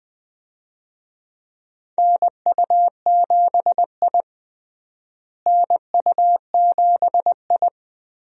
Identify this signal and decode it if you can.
Morse code, heard as a keyed tone, about 20 wpm, 700 Hz. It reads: NU7I NU7I
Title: USA Call Signs - 4 Characters - Repeated 2x - 20wpm